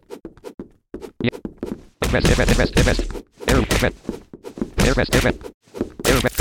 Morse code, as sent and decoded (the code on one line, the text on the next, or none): none